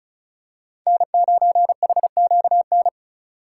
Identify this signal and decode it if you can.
Morse code, heard as a keyed tone, about 34 wpm, 700 Hz. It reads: N9HQN
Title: USA Call Signs - 5 Characters - 6x Character Spacing - 35wpm